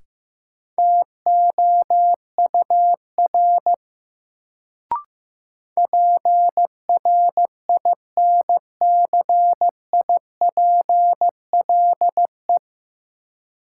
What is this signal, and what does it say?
Morse code, heard as a keyed tone, about 16 wpm, 700 Hz. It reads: TOUR PRINCIPLE
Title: Sentences from Top 1400 Words - New Word Drill - 15wpm